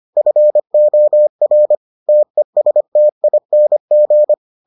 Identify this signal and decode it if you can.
Morse code, heard as a keyed tone, about 25 wpm, 600 Hz. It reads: FOR TESTING